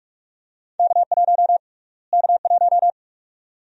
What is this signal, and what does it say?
Morse code, heard as a keyed tone, about 46 wpm, 700 Hz. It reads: K1 K1
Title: USA Call Sign Prefixes - Repeated 2x - 45wpm